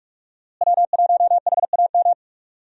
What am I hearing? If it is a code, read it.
Morse code, heard as a keyed tone, about 46 wpm, 700 Hz. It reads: W1HAM
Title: USA Call Signs - 5 Characters - Repeated 2x - 45wpm